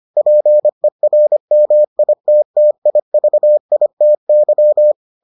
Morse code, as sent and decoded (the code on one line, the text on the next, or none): .--. .-. -- .. - - .. ...- .. - -.--
PERMITTIVITY